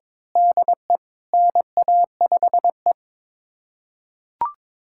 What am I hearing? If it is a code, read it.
Morse code, heard as a keyed tone, about 22 wpm, 700 Hz. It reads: DE NA5E
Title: USA Callsign DE Callsign - 4 Characters - 22wpm